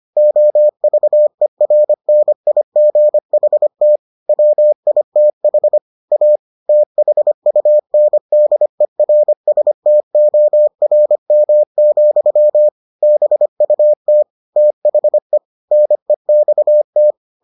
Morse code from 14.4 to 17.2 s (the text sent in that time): THE NEXT